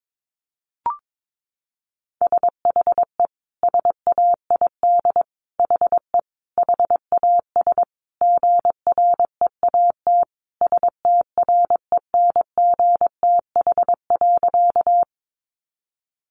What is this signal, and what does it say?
Morse code, heard as a keyed tone, about 22 wpm, 700 Hz. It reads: SHE SAID HE HAS GREAT STRENGTH.
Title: Sentences from Top 1300 Words 22wpm